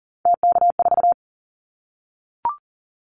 Morse code, heard as a keyed tone, about 40 wpm, 700 Hz. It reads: TK4
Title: Callsign Prefixes for Top 100 DXCC - 40wpm